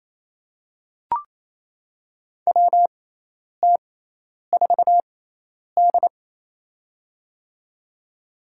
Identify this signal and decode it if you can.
Morse code, heard as a keyed tone, about 28 wpm, spaced as for 10 wpm, 700 Hz. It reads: WT4D
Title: USA Call Signs - 4 Characters - 6x Character Spacing - 28wpm